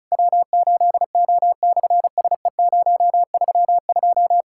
Morse code, heard as a keyed tone, about 33 wpm, 700 Hz. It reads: W8O/SE032
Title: SOTA QSO Practice 35wpm